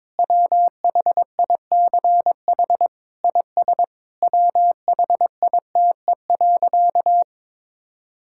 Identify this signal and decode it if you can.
Morse code, heard as a keyed tone, about 22 wpm, 700 Hz. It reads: WHICH IS WHITE.
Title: Sentences from Top 1600 Words 22wpm